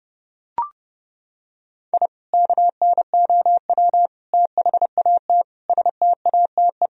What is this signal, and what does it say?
Morse code, heard as a keyed tone, about 30 wpm, 700 Hz. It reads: I KNOW THAT STATE
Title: Sentences from Top 1000 Words - Review Part 2 - 30wpm